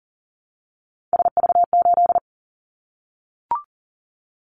Morse code, heard as a keyed tone, about 42 wpm, 700 Hz. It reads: SV8
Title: Callsign Prefixes for Top 50 DXCC - 40wpm